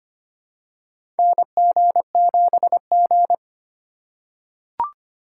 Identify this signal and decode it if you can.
Morse code, heard as a keyed tone, about 25 wpm, 700 Hz. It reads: NG7G
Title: USA Call Signs - 4 Characters - 2x Character Spacing - 25wpm